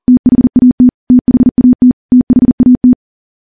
Morse code, heard as a keyed tone, about 40 wpm, 250 Hz. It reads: THAT THAT THAT
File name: morse / morse-training-40WPM